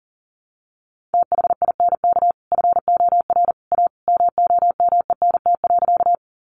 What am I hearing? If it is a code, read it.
Morse code, heard as a keyed tone, about 40 wpm, 700 Hz. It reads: THINK FOR A MOMENT.
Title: Sentences from Top 1000 Words 40wpm